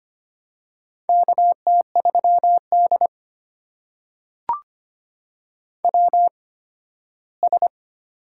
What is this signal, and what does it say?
Morse code, heard as a keyed tone, about 25 wpm, spaced as for 15 wpm, 700 Hz. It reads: KT3D WS
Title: USA Call Signs - 4 Characters - 8x Character Spacing - 25wpm